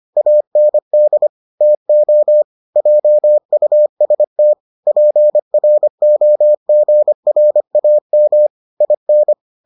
Morse code, heard as a keyed tone, about 25 wpm, 600 Hz. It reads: AND TO JUST PROGRAM IN